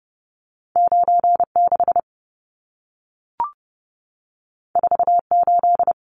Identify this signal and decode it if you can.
Morse code, heard as a keyed tone, about 30 wpm, 700 Hz. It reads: 96 48
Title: Numbers - 2-Digit 30wpm